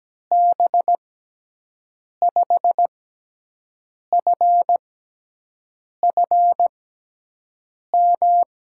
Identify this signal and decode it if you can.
Morse code, heard as a keyed tone, about 17 wpm, spaced as for 6 wpm, 700 Hz. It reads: B5FFM
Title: USA Call Signs - 6 Characters - 6x Character Spacing - 17wpm